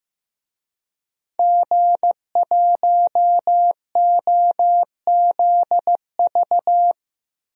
Morse code, heard as a keyed tone, about 15 wpm, 700 Hz. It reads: G1OZV
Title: DXCC-100 Call Signs - 5 Characters - Repeated 2x - 15wpm